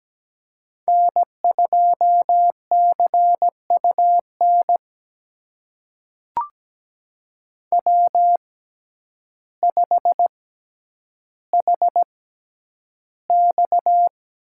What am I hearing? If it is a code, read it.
Morse code, heard as a keyed tone, about 17 wpm, spaced as for 9 wpm, 700 Hz. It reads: N2CUN W5HX
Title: USA Call Signs - 6x Character Spacing - 17wpm